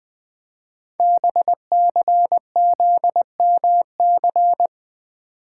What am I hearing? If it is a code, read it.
Morse code, heard as a keyed tone, about 20 wpm, 700 Hz. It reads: BCZMC
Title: Code Groups - 5 Letters - 20wpm